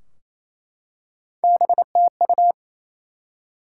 Morse code, heard as a keyed tone, about 28 wpm, 700 Hz. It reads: BTU
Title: Top 180 QSO Elements 28wpm